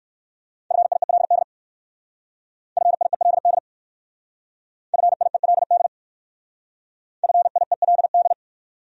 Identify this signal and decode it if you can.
Morse code, heard as a keyed tone, about 55 wpm, 700 Hz. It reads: FIELD FIELD FIELD FIELD
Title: Speed-Racing Top 500 Words 45wpm